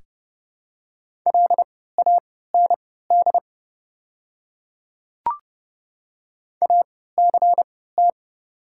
Morse code, heard as a keyed tone, about 30 wpm, spaced as for 17 wpm, 700 Hz. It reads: LAND ACT
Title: 3x Character Spacing - Top 300 Words - 30wpm